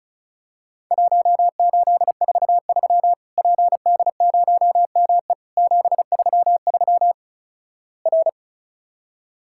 Morse code, heard as a keyed tone, about 34 wpm, 700 Hz. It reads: 1843 PD0ME 733 R
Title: Worked All Europe DX Contest - QTC Exchange - 35wpm